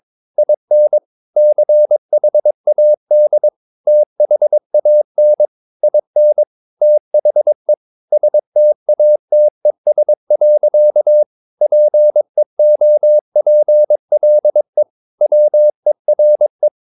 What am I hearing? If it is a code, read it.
Morse code, heard as a keyed tone, about 22 wpm, 600 Hz. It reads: IN CHAD THAN IN THE STATES. PEOPLE WERE
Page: Romanen Never af Ken Follet på engelsk.